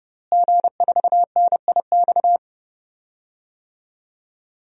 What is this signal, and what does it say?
Morse code, heard as a keyed tone, about 32 wpm, 700 Hz. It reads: G4NIX
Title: DXCC-100 Call Signs - 5 Characters - Repeated 3x - 30wpm